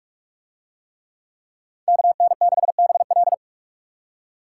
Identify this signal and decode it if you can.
Morse code, heard as a keyed tone, about 48 wpm, 700 Hz. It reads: KN6BL